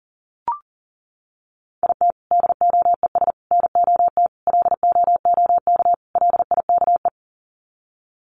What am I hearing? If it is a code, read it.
Morse code, heard as a keyed tone, about 40 wpm, 700 Hz. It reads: IT DOES NOT LOOK LIKE
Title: Sets of 5 Words Encore2 40wpm